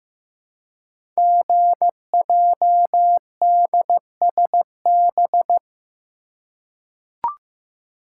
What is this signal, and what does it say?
Morse code, heard as a keyed tone, about 15 wpm, 690 Hz. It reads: GJDSB
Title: Code Groups - 5 Letters - 15wpm